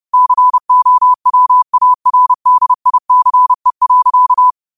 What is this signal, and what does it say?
Morse code, heard as a keyed tone, about 30 wpm, 1000 Hz. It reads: COWARDICE.
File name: qotdfiles / 30wpm